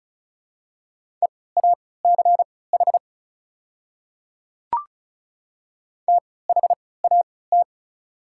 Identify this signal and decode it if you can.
Morse code, heard as a keyed tone, about 36 wpm, spaced as for 20 wpm, 700 Hz. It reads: EACH THAT
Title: 3x Character Spacing - Top 100 Words - 35wpm